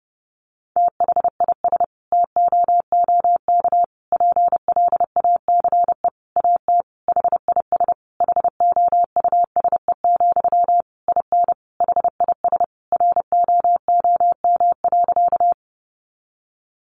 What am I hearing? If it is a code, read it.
Morse code, heard as a keyed tone, about 30 wpm, 700 Hz. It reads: THIS TOOK PLACE AT HIS HOUSE, IN HIS ROOM.
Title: Sentences from Top 300 Words 30wpm